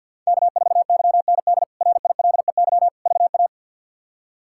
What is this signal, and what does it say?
Morse code, heard as a keyed tone, about 49 wpm, 700 Hz. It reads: K4YND RILEY VA
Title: K1USN SST QSO Practice - 50wpm